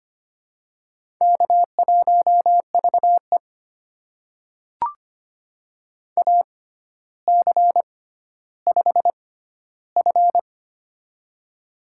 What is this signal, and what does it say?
Morse code, heard as a keyed tone, about 25 wpm, spaced as for 13 wpm, 700 Hz. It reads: K1VE AC5F
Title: USA Call Signs - 4 Characters - 6x Character Spacing - 25wpm